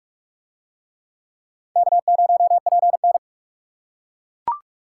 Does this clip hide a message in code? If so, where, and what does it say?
Yes: Morse code from 1.8 to 3.2 s, heard as a keyed tone, about 42 wpm, 700 Hz. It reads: K0PN